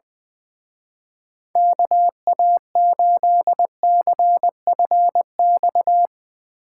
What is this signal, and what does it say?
Morse code, heard as a keyed tone, about 22 wpm, 700 Hz. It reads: KA8CFX
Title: Speed Racing - USA Call Signs 20wpm